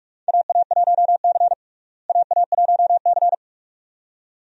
Morse code, heard as a keyed tone, about 47 wpm, 700 Hz. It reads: AA1C AA1C
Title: USA Call Signs - 4 Characters - Repeated 2x - 45wpm